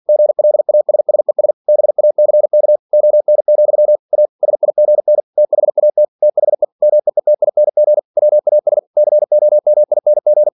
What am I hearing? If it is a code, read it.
Morse code, heard as a keyed tone, about 47 wpm, 600 Hz. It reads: GLASSES BACK ON, A SIGN THAT THE MEETING WAS COMING
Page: Never 47wpm